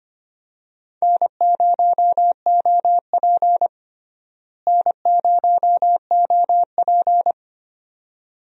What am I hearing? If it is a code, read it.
Morse code, heard as a keyed tone, about 25 wpm, 700 Hz. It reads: N0OP N0OP
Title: USA Call Signs - 4 Characters - Repeated 2x - 25wpm